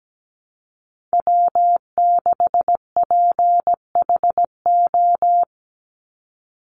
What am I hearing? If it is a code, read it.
Morse code, heard as a keyed tone, about 17 wpm, 700 Hz. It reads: W6PHO
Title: Call Signs - 17wpm